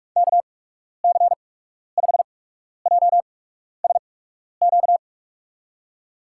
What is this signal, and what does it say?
Morse code, heard as a keyed tone, about 46 wpm, spaced as for 12 wpm, 700 Hz. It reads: KC5JSQ
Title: USA Call Signs - 8x Character Spacing - 45wpm